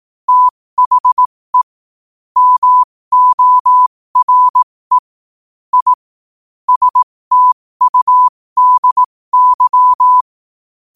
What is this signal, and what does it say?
Morse code, heard as a keyed tone, about 18 wpm, spaced as for 15 wpm, 1000 Hz. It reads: THE MORE I STUDY
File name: qotdfiles / 15wpm